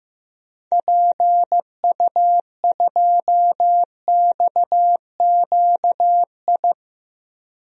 Morse code, heard as a keyed tone, about 15 wpm, 700 Hz. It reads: PU2XQI
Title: DXCC-100 Call Signs - 6 Characters - 15wpm